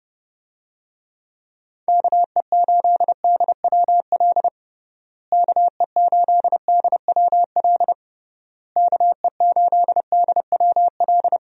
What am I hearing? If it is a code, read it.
Morse code, heard as a keyed tone, about 32 wpm, 700 Hz. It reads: KE8DWL KE8DWL KE8DWL
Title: USA Call Signs - 6 Characters - Repeated 3x - 30wpm